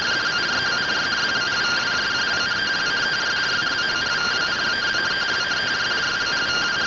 Two-tone teleprinter signal, about 170 Hz shift.